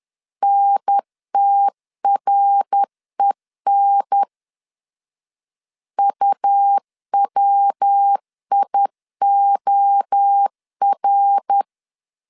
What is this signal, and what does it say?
Morse code, heard as a keyed tone, about 10 wpm, 800 Hz. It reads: NTREN UWIOR
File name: CW / DET / 1 3 1500C S C 10wpm 3335